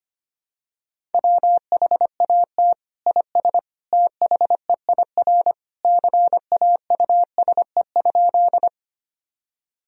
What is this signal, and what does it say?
Morse code, heard as a keyed tone, about 25 wpm, 700 Hz. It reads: WHAT IS THEIR CAUSE?
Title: Sentences from Top 1000 Words - Review Part 1 - 25wpm